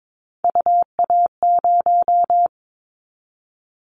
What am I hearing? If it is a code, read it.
Morse code, heard as a keyed tone, about 22 wpm, 700 Hz. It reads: UA0
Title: Callsign Prefixes for Top 100 DXCC - 22wpm